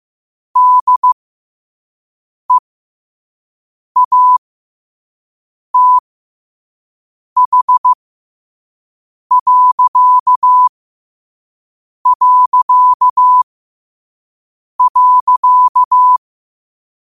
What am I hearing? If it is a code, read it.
Morse code, heard as a keyed tone, about 15 wpm, spaced as for 5 wpm, 1000 Hz. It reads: DEATH...
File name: qotdfiles / 5wpm